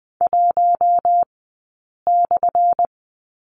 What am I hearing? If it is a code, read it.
Morse code, heard as a keyed tone, about 20 wpm, 700 Hz. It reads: 1 /
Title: 217 Lesson 35d - Number 0 - Two Characters 20wpm